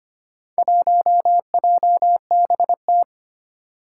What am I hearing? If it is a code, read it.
Morse code, heard as a keyed tone, about 25 wpm, 700 Hz. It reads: 1JBT
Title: Four Challenging Letter-Number Combinations 25wpm